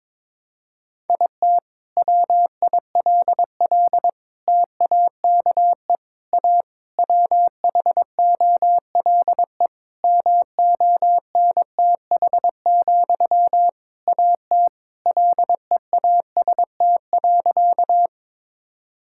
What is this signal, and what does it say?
Morse code, heard as a keyed tone, about 21 wpm, 700 Hz. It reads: IT WILL TAKE A WHOLE MONTH, AT LEAST.
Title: Sentences from Top 1000 Words - Review Part 5 - 22wpm